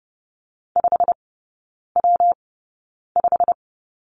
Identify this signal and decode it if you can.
Morse code, heard as a keyed tone, about 30 wpm, spaced as for 9 wpm, 700 Hz. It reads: 5W5
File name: Lesson 19e - Letter w - Three Characters 30wpm